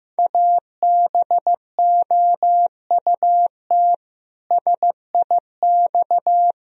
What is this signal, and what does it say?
Morse code, heard as a keyed tone, about 15 wpm, 700 Hz. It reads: ABOUT SIX
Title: Sentences from Top 1000 Words - Review Part 2 - 15wpm